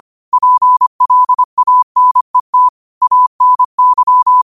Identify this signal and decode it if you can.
Morse code, heard as a keyed tone, about 25 wpm, 1000 Hz. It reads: PLANET ANY